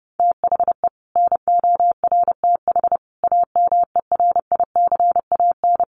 Morse code, heard as a keyed tone, about 30 wpm, 700 Hz. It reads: THE NORTH AMERICAN
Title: Sets of 5 Words Encore 30wpm